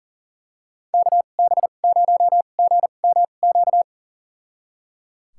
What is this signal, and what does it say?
Morse code, heard as a keyed tone, about 38 wpm, spaced as for 28 wpm, 700 Hz. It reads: KB0GMQ
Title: USA Call Signs - 2x Character Spacing - 40wpm